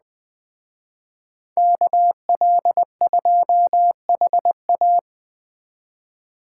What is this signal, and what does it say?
Morse code, heard as a keyed tone, about 20 wpm, 700 Hz. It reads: KL2HA
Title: Speed Racing - USA 2x2 Call Signs 17wpm